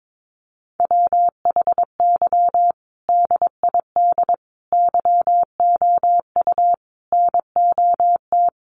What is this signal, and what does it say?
Morse code, heard as a keyed tone, about 22 wpm, 700 Hz. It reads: WHY DID YOU NOT
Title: Sentences from Top 1100 Words 22wpm